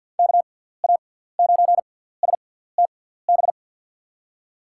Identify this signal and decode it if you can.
Morse code, heard as a keyed tone, about 53 wpm, spaced as for 17 wpm, 700 Hz. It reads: KA9STB
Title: USA Call Signs - 6 Characters - 6x Character Spacing - 50wpm